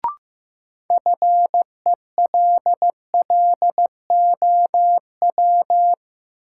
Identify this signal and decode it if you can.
Morse code, heard as a keyed tone, about 15 wpm, 700 Hz. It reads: FELLOW
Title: Sets of 5 Words Encore - New Word Drill - 15wpm